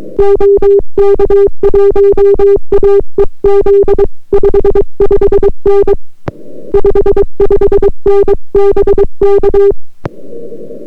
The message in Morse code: OK1AEZ 55N 55N BK